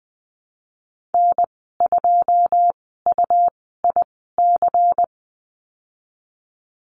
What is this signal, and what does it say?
Morse code, heard as a keyed tone, about 20 wpm, spaced as for 15 wpm, 700 Hz. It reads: N2UIC
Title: 128 Lesson 21g - Number 6 - Call Signs 20wpm